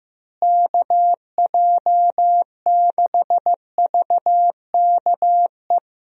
Morse code, heard as a keyed tone, about 16 wpm, 700 Hz. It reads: KJ6VKE
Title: USA Call Signs - 6 Characters - Repeated 3x - 15wpm